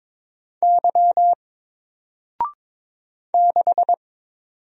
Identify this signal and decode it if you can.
Morse code, heard as a keyed tone, about 22 wpm, 700 Hz. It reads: Y 6
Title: Single Letter-Number Alt 22wpm